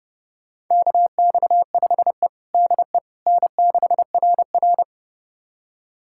Morse code, heard as a keyed tone, about 32 wpm, 700 Hz. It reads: KX5E DE N6RR
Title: USA Callsign DE Callsign - 4 Characters - 30wpm